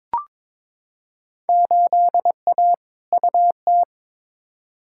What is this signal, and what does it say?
Morse code, heard as a keyed tone, about 22 wpm, 700 Hz. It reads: 8A UT
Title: Field Day Exchanges Encore - 22wpm